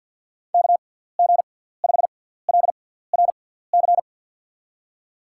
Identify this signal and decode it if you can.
Morse code, heard as a keyed tone, about 49 wpm, spaced as for 17 wpm, 700 Hz. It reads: KG5LRC
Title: USA Call Signs - 6 Characters - 6x Character Spacing - 50wpm